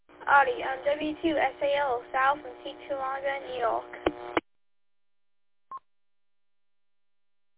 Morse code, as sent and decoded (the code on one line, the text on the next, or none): none